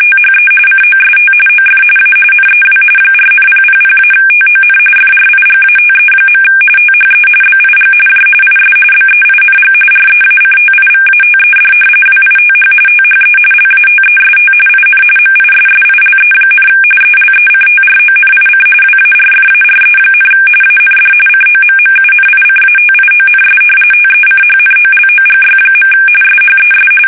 RTTY, about 45.45 baud. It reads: THE COST OF A NEW 4,000-TON FRIGATE. THE BIG J IS BEING REACTIVATED WITH THE LATEST IN MISSILE, ELECTRONIC WARFARE, AND COMMUNICATIONS TECHNOLOGY. SHES
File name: fsk850 / battleshipnj